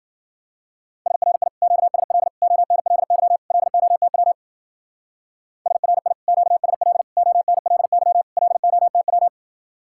Morse code, heard as a keyed tone, about 48 wpm, 700 Hz. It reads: SRI QSL ONLY LOTW SRI QSL ONLY LOTW
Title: Speed-Racing Common QSO Phrases - 40wpm